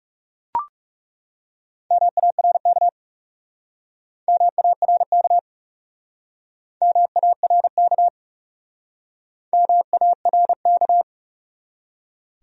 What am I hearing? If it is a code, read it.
Morse code, heard as a keyed tone, about 40 wpm, 700 Hz. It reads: MARK MARK MARK MARK